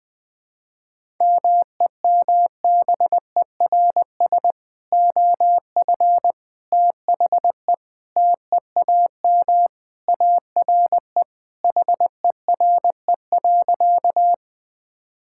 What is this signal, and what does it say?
Morse code, heard as a keyed tone, about 20 wpm, 700 Hz. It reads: MEMBERS OF THE TEAM ARE HERE.
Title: Sentences with 3-Word Phrases - Spoken Beginning - 20wpm